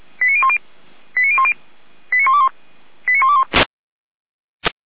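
Five-tone selective call (ZVEI), heard as quick runs of five tones, tones of about 70 ms. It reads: 89010 89010 89121 89121
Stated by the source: ZVEI five-tone call 89060 received by SDR.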